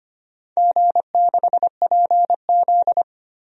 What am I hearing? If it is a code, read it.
Morse code, heard as a keyed tone, about 25 wpm, 700 Hz. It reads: G6PZ
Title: DXCC-100 Call Signs - 4 Characters - 25wpm